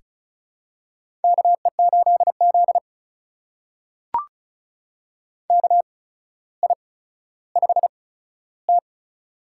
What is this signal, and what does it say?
Morse code, heard as a keyed tone, about 36 wpm, spaced as for 15 wpm, 700 Hz. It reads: KE8Z KI5T
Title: USA Call Signs - 8x Character Spacing - 35wpm